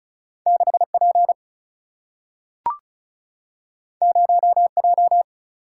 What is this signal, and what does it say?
Morse code, heard as a keyed tone, about 36 wpm, 700 Hz. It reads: 6 P 0 J
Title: Single-Letter Single-Number 35wpm